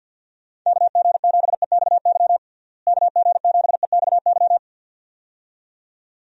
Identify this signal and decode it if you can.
Morse code, heard as a keyed tone, about 51 wpm, 700 Hz. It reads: KG7EXY KG7EXY
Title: USA Call Signs - 6 Characters - Repeated 2x - 50wpm